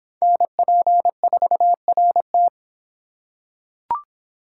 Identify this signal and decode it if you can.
Morse code, heard as a keyed tone, about 23 wpm, 700 Hz. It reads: NP4RT